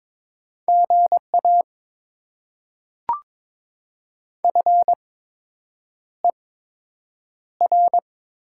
Morse code, heard as a keyed tone, about 22 wpm, spaced as for 8 wpm, 700 Hz. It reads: GA FER
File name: Top 20 QSO Elements - 8x Character Spacing - 22wpm